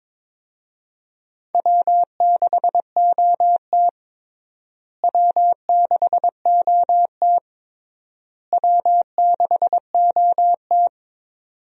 Morse code, heard as a keyed tone, about 22 wpm, 700 Hz. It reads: W6OT W6OT W6OT
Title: USA Call Signs - 4 Characters - Repeated 3x - 22wpm